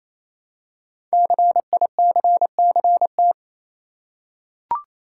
Morse code, heard as a keyed tone, about 27 wpm, 700 Hz. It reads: CICCT
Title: Code Groups - 5 Letters - 28wpm